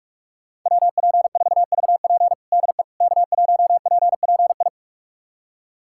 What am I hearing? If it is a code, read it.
Morse code, heard as a keyed tone, about 41 wpm, 700 Hz. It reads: WP4VP DE K1PPI
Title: USA Callsign DE Callsign - 5 Characters - 45wpm